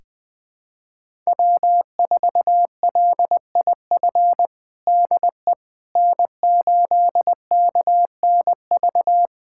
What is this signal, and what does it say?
Morse code, heard as a keyed tone, about 20 wpm, 700 Hz. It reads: W4LIF DE N8KNV
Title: USA Callsign DE Callsign - 5 Characters - 20wpm